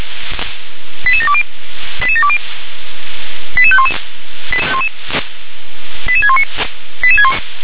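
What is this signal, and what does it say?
Five-tone selective call (ZVEI), heard as quick runs of five tones, tones of about 70 ms. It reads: 88720 88720 88511 88619
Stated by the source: ZVEI five-tone selective call received on an SDR.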